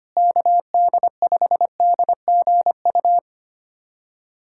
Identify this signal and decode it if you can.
Morse code, heard as a keyed tone, about 25 wpm, 700 Hz. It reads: KD5DGU
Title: USA Callsign DE Callsign 25wpm